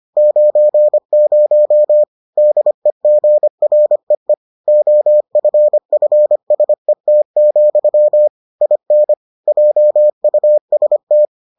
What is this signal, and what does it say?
Morse code, heard as a keyed tone, about 25 wpm, 600 Hz. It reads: 90 DEGREE OFFSET, IN JUST